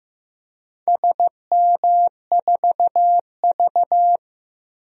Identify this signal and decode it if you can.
Morse code, heard as a keyed tone, about 15 wpm, 700 Hz. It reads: SM4V
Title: DXCC-100 Call Signs - 4 Characters - 15wpm